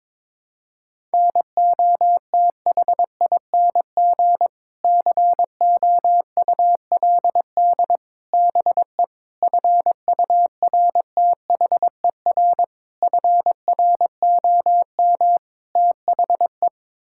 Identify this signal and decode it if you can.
Morse code, heard as a keyed tone, about 21 wpm, 700 Hz. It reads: NOTHING COULD BE FURTHER FROM THE
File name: practice / 22wpm / Sets of 6 Words - 22wpm